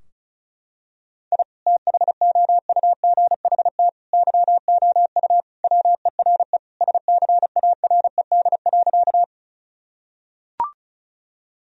Morse code, heard as a keyed tone, about 36 wpm, 700 Hz. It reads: I THOUGHT YOU WERE SCARED.
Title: Sentences from Top 1000 Words - Review Part 4 - 35wpm